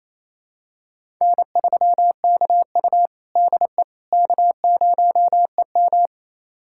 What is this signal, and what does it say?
Morse code, heard as a keyed tone, about 27 wpm, 700 Hz. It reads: N3KU DE K0EM
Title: USA Callsign DE Callsign - 4 Characters - 28wpm